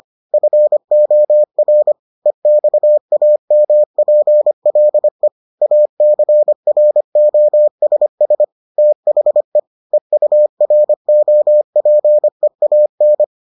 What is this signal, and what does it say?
Morse code, heard as a keyed tone, about 25 wpm, 600 Hz. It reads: FOR EXAMPLE ACROSS THE EUROPEAN